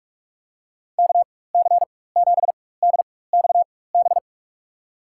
Morse code, heard as a keyed tone, about 42 wpm, spaced as for 20 wpm, 700 Hz. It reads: KC7DXB
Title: USA Call Signs - 4x Character Spacing - 45wpm